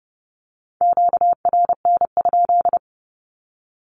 Morse code, heard as a keyed tone, about 30 wpm, 700 Hz. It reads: QRN?
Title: Q-Codes 30wpm